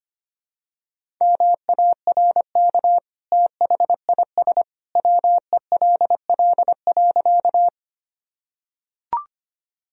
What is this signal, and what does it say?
Morse code, heard as a keyed tone, about 25 wpm, 700 Hz. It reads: MARK THIS WELL.
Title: Sentences from Top 1000 Words - Review Part 1 - 25wpm